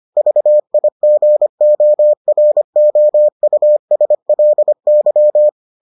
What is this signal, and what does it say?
Morse code, heard as a keyed tone, about 25 wpm, 600 Hz. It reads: VIGOROUSLY